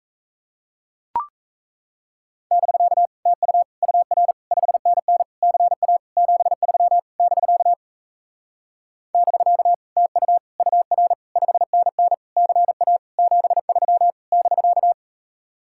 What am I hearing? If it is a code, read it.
Morse code, heard as a keyed tone, about 38 wpm, 700 Hz. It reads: <BK> TU UR 5NN CA 73 <BK> <BK> TU UR 5NN CA 73 <BK>